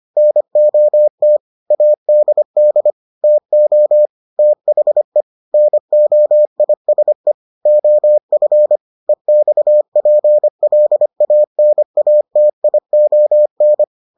Morse code, heard as a keyed tone, about 25 wpm, 600 Hz. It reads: NOT ADD TO THE NOISE OF EXPLANATION